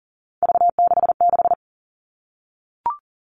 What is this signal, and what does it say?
Morse code, heard as a keyed tone, about 40 wpm, 700 Hz. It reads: V66
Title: All Cut Numbers - 3-Digit 40wpm